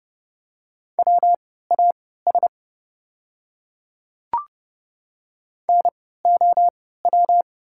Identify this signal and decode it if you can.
Morse code, heard as a keyed tone, about 30 wpm, spaced as for 17 wpm, 700 Hz. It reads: WAS NOW
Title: 3x Character Spacing - Top 100 Words - 30wpm